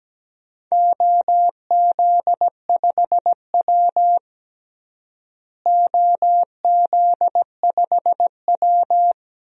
Morse code, heard as a keyed tone, about 17 wpm, 700 Hz. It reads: OZ5W OZ5W
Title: DXCC-100 Call Signs - 4 Characters - Repeated 2x - 17wpm